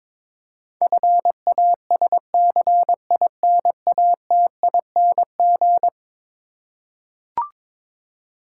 Morse code, heard as a keyed tone, about 22 wpm, 700 Hz. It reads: FASCINATING
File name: practice / 22wpm / Eleven Letter Words - 22wpm